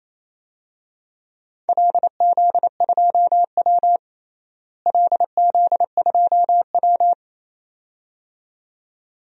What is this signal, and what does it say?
Morse code, heard as a keyed tone, about 28 wpm, 700 Hz. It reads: LZ2W LZ2W
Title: DXCC-100 Call Signs - 4 Characters - Repeated 2x - 28wpm